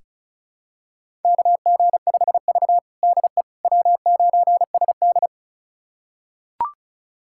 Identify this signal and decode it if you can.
Morse code, heard as a keyed tone, about 34 wpm, 700 Hz. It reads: KG5V DE W9SD
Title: USA Callsign DE Callsign - 4 Characters - 35wpm